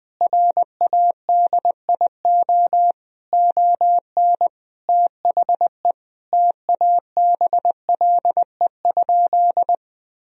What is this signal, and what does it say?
Morse code, heard as a keyed tone, about 20 wpm, 700 Hz. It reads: RADIO ON THE TABLE?
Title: Sentences from Top 1000 Words - Review Part 3 - 20wpm